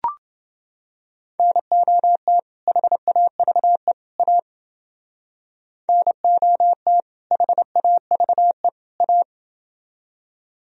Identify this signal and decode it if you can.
Morse code, heard as a keyed tone, about 29 wpm, 700 Hz. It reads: NOT HAVE A NOT HAVE A